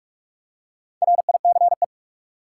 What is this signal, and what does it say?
Morse code, heard as a keyed tone, about 44 wpm, 700 Hz. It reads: RICE
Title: Four Letter Words - 45wpm